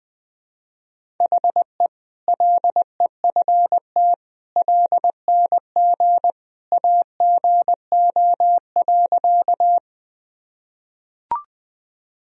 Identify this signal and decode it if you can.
Morse code, heard as a keyed tone, about 20 wpm, 700 Hz. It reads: HE LEFT LNG AGO.
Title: Sentences from Top 500 Words and Abbreviations 20wpm